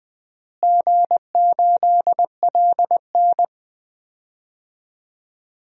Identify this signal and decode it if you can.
Morse code, heard as a keyed tone, about 20 wpm, 700 Hz. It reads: G8LN
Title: DXCC-100 Call Signs - 4 Characters - Repeated 2x - 20wpm